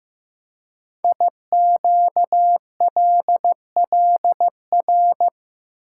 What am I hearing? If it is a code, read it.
Morse code, heard as a keyed tone, about 15 wpm, 700 Hz. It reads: IQLLR
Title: Code Groups - 5 Letters - 15wpm